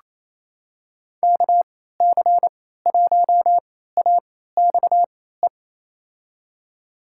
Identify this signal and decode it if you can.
Morse code, heard as a keyed tone, about 28 wpm, spaced as for 16 wpm, 700 Hz. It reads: KC1AXE
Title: USA Call Signs - 6 Characters - 3x Character Spacing - 28wpm